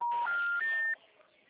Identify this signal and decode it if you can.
Special information tone, with noise.